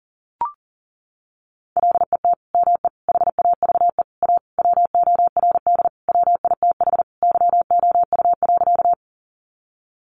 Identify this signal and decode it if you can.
Morse code, heard as a keyed tone, about 37 wpm, 700 Hz. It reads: LET ME HAVE A WORD WITH YOU.
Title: Sentences from Top 300 Words 40wpm